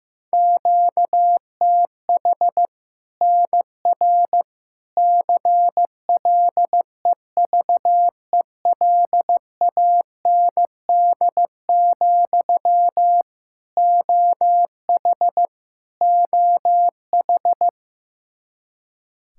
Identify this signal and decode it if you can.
Morse code, heard as a keyed tone, about 15 wpm, 700 Hz. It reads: QTH NR CLEVELAND, OH OH
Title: Practice QSO - Exchange 1 - Beginning - 15wpm